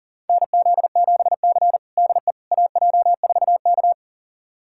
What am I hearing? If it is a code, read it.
Morse code, heard as a keyed tone, about 40 wpm, 700 Hz. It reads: NZ7C DE AJ4K
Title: USA Callsign DE Callsign - 4 Characters - 40wpm